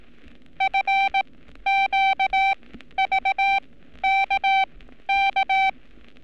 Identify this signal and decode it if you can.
Morse code, heard as a keyed tone, about 18 wpm, spaced as for 12 wpm, 780 Hz. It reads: FQVKK
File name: training / NAVPERS 11224-11235 / NAVPERS 11227 RA RB